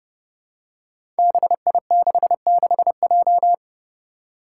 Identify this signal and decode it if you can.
Morse code, heard as a keyed tone, about 30 wpm, 700 Hz. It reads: BI66J
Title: Five Challenging Letter-Number Combinations 30wpm